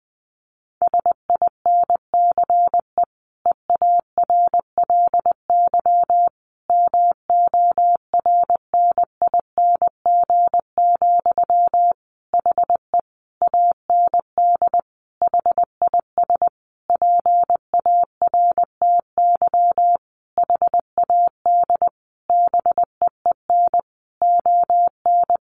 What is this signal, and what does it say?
Morse code, heard as a keyed tone, about 19 wpm, 700 Hz. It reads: SINCE EARLY MORNING, HE AND HIS PARTY HAD BEEN ON
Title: Sentences from Top 1400 Words 20wpm